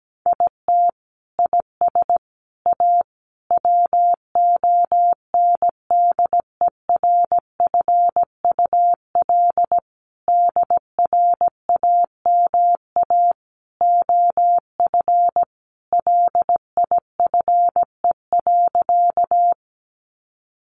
Morse code, heard as a keyed tone, about 17 wpm, 700 Hz. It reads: IT IS A WONDERFUL DRAMA OF LIFE.
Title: Sentences from Top 1500 Words 17wpm